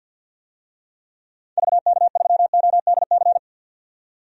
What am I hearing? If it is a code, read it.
Morse code, heard as a keyed tone, about 52 wpm, 700 Hz. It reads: VK3ODC
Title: DXCC-100 Call Signs - 6 Characters - 50wpm